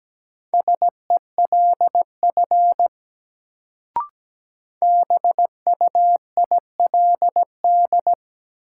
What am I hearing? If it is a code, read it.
Morse code, heard as a keyed tone, about 17 wpm, 700 Hz. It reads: SELF BUILD
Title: Sentences from Top 1000 Words - Review Part 1 - New Word Drill - 17wpm